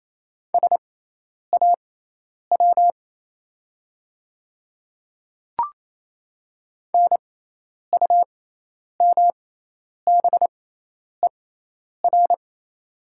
Morse code, heard as a keyed tone, about 27 wpm, spaced as for 10 wpm, 690 Hz. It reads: SAW NUMBER
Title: 6x Character Spacing - Top 500 Words - 28wpm